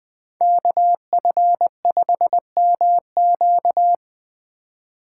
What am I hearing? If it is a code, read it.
Morse code, heard as a keyed tone, about 20 wpm, 700 Hz. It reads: KF5MQ
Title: Speed Racing - USA 2x2 Call Signs 17wpm